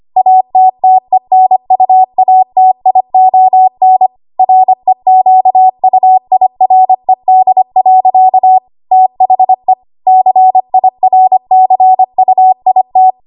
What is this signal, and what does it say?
Morse code, heard as a keyed tone, about 25 wpm, 750 Hz. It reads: ATTENUATION REQUIRED. THE CIRCUIT
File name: Archive / 25 WPM / 25WPM